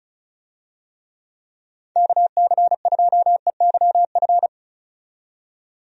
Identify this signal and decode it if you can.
Morse code, heard as a keyed tone, about 34 wpm, 700 Hz. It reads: KC2EYF
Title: USA Call Signs - 6 Characters - 35wpm